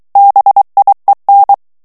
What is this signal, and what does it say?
Morse code, heard as a keyed tone, about 23 wpm, 800 Hz. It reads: BIEN
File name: morse mp3 / FRANCAIS TEXTE CLAIR 2 VITESSE 22 MOTS MINUTES